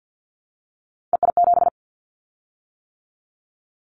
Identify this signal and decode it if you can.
Morse code, heard as a keyed tone, about 50 wpm, 700 Hz. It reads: EI7
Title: Callsign Prefixes for Top 50 DXCC - 50wpm